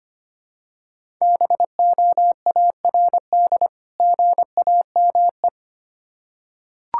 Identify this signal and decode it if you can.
Morse code, heard as a keyed tone, about 25 wpm, 700 Hz. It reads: BOARD GAME
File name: practice / 25wpm / Difficult Words with Hints - 25wpm